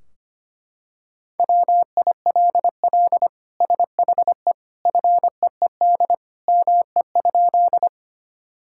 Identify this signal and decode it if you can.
Morse code, heard as a keyed tone, about 25 wpm, 700 Hz. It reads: WILL SHE FEED ME?
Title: Sentences from Top 1600 Words 25wpm